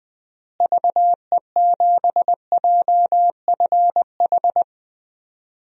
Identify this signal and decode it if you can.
Morse code, heard as a keyed tone, about 20 wpm, 700 Hz. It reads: VE7JFH
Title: DXCC-100 Call Signs - 6 Characters - Repeated 2x - 20wpm